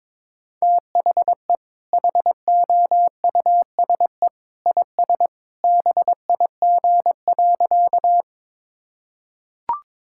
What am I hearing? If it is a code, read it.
Morse code, heard as a keyed tone, about 22 wpm, 700 Hz. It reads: THE HOUSE IS BIG.
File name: practice / 22wpm / Sentences with 2-Word Phrases - 22wpm